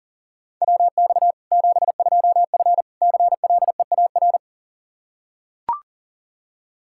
Morse code, heard as a keyed tone, about 40 wpm, 700 Hz. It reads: WX 72F CLEAR
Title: Practice QSO - Exchange 2 - Info - 40wpm